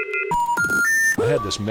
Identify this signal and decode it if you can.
Special information tone, with speech.